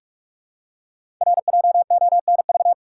Morse code, heard as a keyed tone, about 44 wpm, 700 Hz. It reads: RJONV
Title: Code Groups - 5 Letters - 45wpm